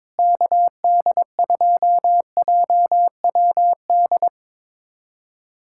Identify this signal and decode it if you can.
Morse code, heard as a keyed tone, about 22 wpm, 700 Hz. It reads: KD2JWD
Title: USA Callsign DE Callsign 22wpm